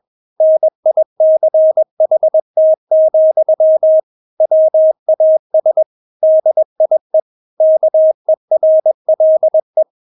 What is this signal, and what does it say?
Morse code, heard as a keyed tone, about 21 wpm, 630 Hz. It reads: NICHT, WAS DIE KERLE
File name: Witz2 21wpm